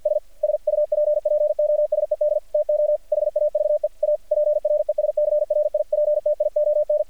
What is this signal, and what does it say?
Morse code, heard as a keyed tone, about 48 wpm, 600 Hz. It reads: S SUPPOSED TO HAVE A PRESCRIPTION
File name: archived / CRSnet